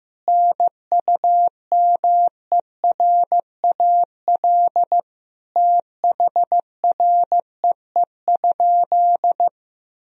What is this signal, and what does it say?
Morse code, heard as a keyed tone, about 15 wpm, 700 Hz. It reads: NUMERAL THREE?
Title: Sentences from Top 1000 Words - Review Part 2 - 15wpm